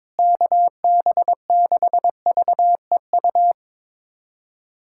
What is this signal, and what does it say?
Morse code, heard as a keyed tone, about 22 wpm, 700 Hz. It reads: KB6VEU
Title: USA Callsign DE Callsign - 6 Characters - 22wpm